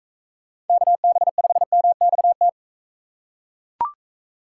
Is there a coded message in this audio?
Yes: Morse code from 0.7 to 2.5 s, heard as a keyed tone, about 38 wpm, 700 Hz. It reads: KB5MXT